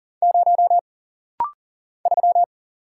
Morse code, heard as a keyed tone, about 39 wpm, 700 Hz. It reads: 0 3
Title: Single Letter-Number - Rapid-Fire 40wpm